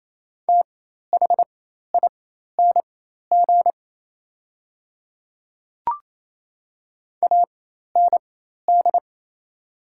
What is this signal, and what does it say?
Morse code, heard as a keyed tone, about 27 wpm, spaced as for 13 wpm, 690 Hz. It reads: THING AND
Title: 4x Character Spacing - Top 100 Words - 28wpm